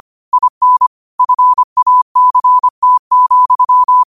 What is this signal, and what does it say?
Morse code, heard as a keyed tone, about 25 wpm, 1000 Hz. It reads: IN FACT,
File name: qotdfiles / 25wpm